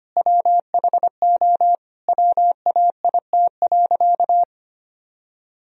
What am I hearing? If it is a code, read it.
Morse code, heard as a keyed tone, about 25 wpm, 700 Hz. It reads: WHO WAIT.
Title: Sentences from Top 1000 Words - Review Part 2 - 25wpm